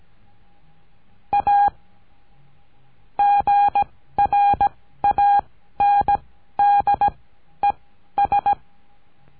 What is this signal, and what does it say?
Morse code, heard as a keyed tone, about 18 wpm, spaced as for 12 wpm, 810 Hz. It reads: A GRANDES
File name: CURSO DE TELEGRAFIA